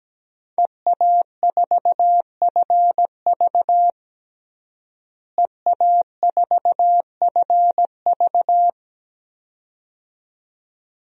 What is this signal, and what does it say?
Morse code, heard as a keyed tone, about 17 wpm, 700 Hz. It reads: EA4FV EA4FV
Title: DXCC-100 Call Signs - 5 Characters - Repeated 2x - 17wpm